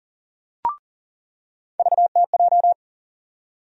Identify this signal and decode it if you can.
Morse code, heard as a keyed tone, about 40 wpm, 700 Hz. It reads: VTJ
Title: USA Call Sign Postfixes - 3 Letters 40wpm